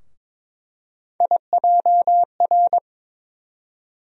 Morse code, heard as a keyed tone, about 21 wpm, 700 Hz. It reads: IJR
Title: USA Call Sign Postfixes - 3 Letters 22wpm